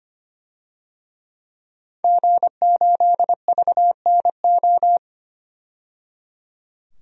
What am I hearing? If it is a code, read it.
Morse code, heard as a keyed tone, about 25 wpm, 700 Hz. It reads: G8VNO
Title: DXCC-100 Call Signs - 5 Characters - 25wpm